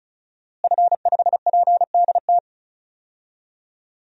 Morse code, heard as a keyed tone, about 36 wpm, 700 Hz. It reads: F5PDT
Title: DXCC-100 Call Signs - 5 Characters - Repeated 2x - 35wpm